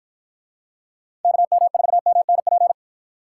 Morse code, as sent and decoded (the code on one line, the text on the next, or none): -.- -- ....- -- -. .--.
KM4MNP